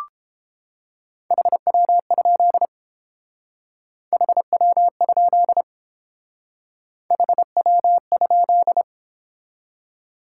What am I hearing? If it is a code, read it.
Morse code, heard as a keyed tone, about 30 wpm, 700 Hz. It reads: HW? HW? HW?